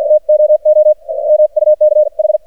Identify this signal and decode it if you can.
Morse code, heard as a keyed tone, about 47 wpm, spaced as for 40 wpm, 610 Hz. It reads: TOO MUKH
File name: archived / CRSnet